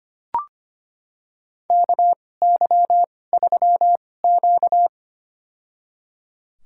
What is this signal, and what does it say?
Morse code, heard as a keyed tone, about 25 wpm, spaced as for 18 wpm, 700 Hz. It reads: KY3Q
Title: USA Call Signs - 2x Character Spacing - 25wpm